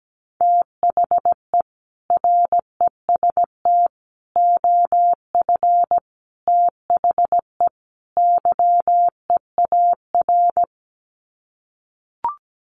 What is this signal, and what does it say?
Morse code, heard as a keyed tone, about 17 wpm, 700 Hz. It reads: THE REST OF THE YEAR
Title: Sets of 5 Words Encore 17wpm